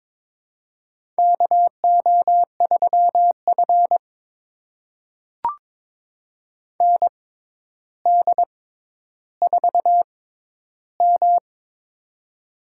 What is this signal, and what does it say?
Morse code, heard as a keyed tone, about 22 wpm, spaced as for 11 wpm, 700 Hz. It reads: KO3F ND4M
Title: USA Call Signs - 4 Characters - 6x Character Spacing - 22wpm